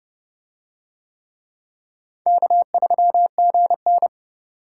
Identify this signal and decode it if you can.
Morse code, heard as a keyed tone, about 27 wpm, 700 Hz. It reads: K3GN